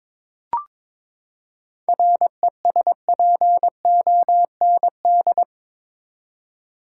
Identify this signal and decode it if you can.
Morse code, heard as a keyed tone, about 22 wpm, 700 Hz. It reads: RESPOND
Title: Seven Letter Words - 22wpm